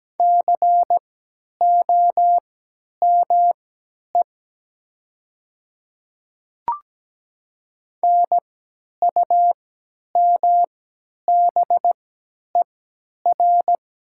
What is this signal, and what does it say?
Morse code, heard as a keyed tone, about 17 wpm, spaced as for 10 wpm, 700 Hz. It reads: COME NUMBER
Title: 3x Character Spacing - Top 100 Words - 17wpm